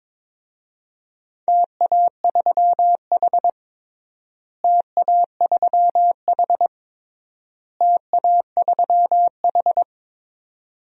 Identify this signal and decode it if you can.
Morse code, heard as a keyed tone, about 21 wpm, 700 Hz. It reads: TA3H TA3H TA3H
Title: DXCC-100 Call Signs - 4 Characters - Repeated 3x - 22wpm